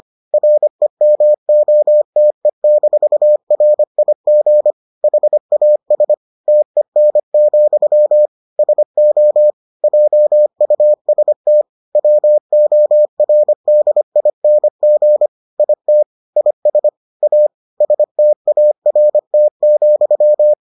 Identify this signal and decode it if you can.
Morse code, heard as a keyed tone, about 25 wpm, 600 Hz. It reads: REMOTE-RIG HAS TEN, SO JUST WORDING IT IS A START,